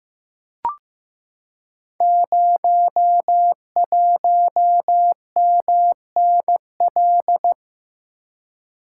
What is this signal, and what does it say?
Morse code, heard as a keyed tone, about 15 wpm, 700 Hz. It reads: 01MNL
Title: Code Groups - 5 Letter:Numbers - 15wpm